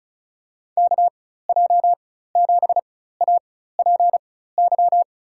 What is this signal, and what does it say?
Morse code, heard as a keyed tone, about 36 wpm, spaced as for 16 wpm, 700 Hz. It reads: KJ7APY
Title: USA Call Signs - 4x Character Spacing - 35wpm